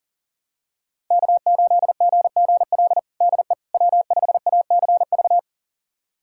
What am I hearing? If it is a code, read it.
Morse code, heard as a keyed tone, about 40 wpm, 700 Hz. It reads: K8GGL DE W5ACV
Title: USA Callsign DE Callsign - 5 Characters - 40wpm